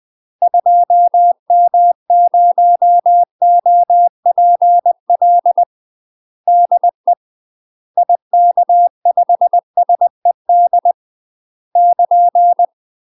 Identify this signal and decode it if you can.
Morse code, heard as a keyed tone, about 20 wpm, 700 Hz. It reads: 2M0OPL DE IK5SED <KN>
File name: UK MASTER QSOs TRAINING / 20WPM / UK MASTER QSOs 0015